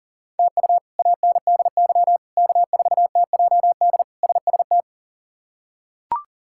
Morse code, heard as a keyed tone, about 40 wpm, 700 Hz. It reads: TU ANDY K4TJD SST
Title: K1USN SST QSO Practice - 40wpm